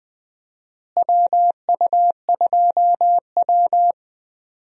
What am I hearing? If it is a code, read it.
Morse code, heard as a keyed tone, about 20 wpm, 700 Hz. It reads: WU2W
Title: USA Call Signs - 4 Characters - Repeated 3x - 20wpm